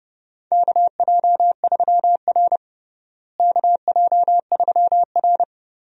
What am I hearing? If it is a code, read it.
Morse code, heard as a keyed tone, about 31 wpm, 700 Hz. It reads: KJ3R KJ3R
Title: USA Call Signs - 4 Characters - Repeated 3x - 30wpm